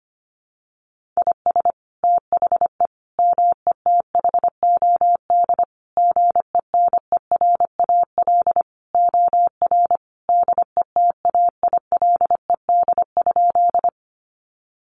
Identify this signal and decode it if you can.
Morse code, heard as a keyed tone, about 25 wpm, 700 Hz. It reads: IS THE METHOD GENERAL OR DETAILED?
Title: Sentences from Top 1500 Words 25wpm